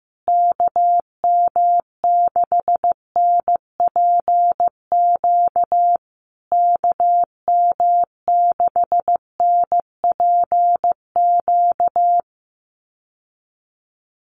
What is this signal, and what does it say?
Morse code, heard as a keyed tone, about 15 wpm, 700 Hz. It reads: KM6NPQ KM6NPQ
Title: USA Call Signs Repeated 3x 15wpm